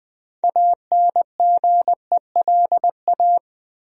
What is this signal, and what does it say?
Morse code, heard as a keyed tone, about 20 wpm, 700 Hz. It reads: ANGELA
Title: USA Common Names - 20wpm